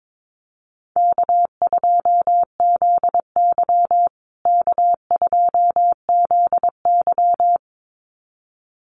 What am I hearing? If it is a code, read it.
Morse code, heard as a keyed tone, about 22 wpm, 700 Hz. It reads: K2ZY K2ZY
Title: USA Call Signs Repeated 2x 22wpm